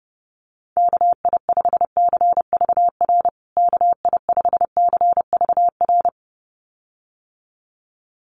Morse code, heard as a keyed tone, about 30 wpm, 700 Hz. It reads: KI5CVR KI5CVR
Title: USA Call Signs Repeated 2x 30wpm